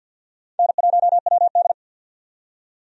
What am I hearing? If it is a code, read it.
Morse code, heard as a keyed tone, about 47 wpm, 700 Hz. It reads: N1WD
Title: USA Call Signs - 4 Characters - 50wpm